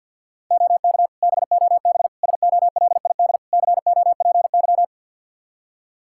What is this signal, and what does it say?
Morse code, heard as a keyed tone, about 52 wpm, 700 Hz. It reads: OK BOB SOLID COPY
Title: Practice QSO - Exchange 3 - More Info - 50wpm